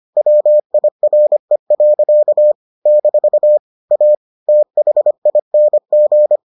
Morse code, heard as a keyed tone, about 25 wpm, 600 Hz. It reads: WIRE. - A THING